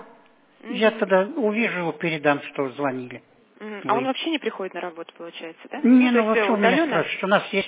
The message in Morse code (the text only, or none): none